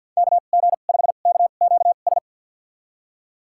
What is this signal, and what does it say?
Morse code, heard as a keyed tone, about 53 wpm, spaced as for 37 wpm, 700 Hz. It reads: KG5KQS
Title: USA Call Signs - 2x Character Spacing - 50wpm